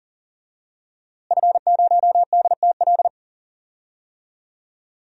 Morse code, heard as a keyed tone, about 39 wpm, 700 Hz. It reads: F0DTL
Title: DXCC-100 Call Signs - 5 Characters - Repeated 3x - 40wpm